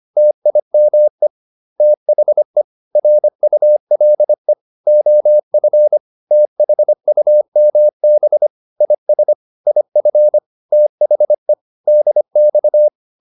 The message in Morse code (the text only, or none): TIME THE RULE OF THUMB IS IF THE DX